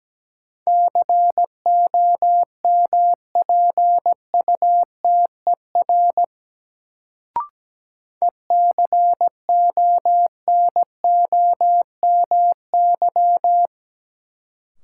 Morse code, heard as a keyed tone, about 17 wpm, 700 Hz. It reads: COMPUTER ECONOMY
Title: Sentences from Top 1000 Words - Review Part 3 - New Word Drill - 17wpm